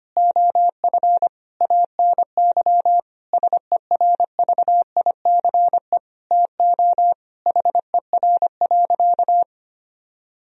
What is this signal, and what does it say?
Morse code, heard as a keyed tone, about 25 wpm, 700 Hz. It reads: OF ANY SERVICE TO HER.
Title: Sentences from Top 1000 Words - Review Part 4 - 25wpm